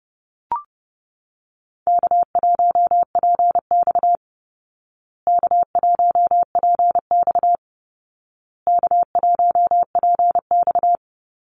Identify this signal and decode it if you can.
Morse code, heard as a keyed tone, about 30 wpm, 700 Hz. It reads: K1PX K1PX K1PX
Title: Easy Call Signs Repeated 3x - 30wpm